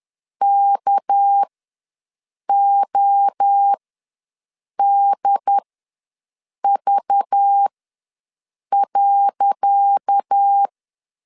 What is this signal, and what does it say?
Morse code, heard as a keyed tone, about 10 wpm, spaced as for 6 wpm, 800 Hz. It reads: KODV.
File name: LEZIONI / 10 SIGNS / 10C DOT ws 10wpm 4213